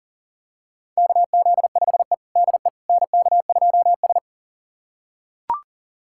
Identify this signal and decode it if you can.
Morse code, heard as a keyed tone, about 37 wpm, 700 Hz. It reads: KZ5E DE NK2S
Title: USA Callsign DE Callsign - 4 Characters - 40wpm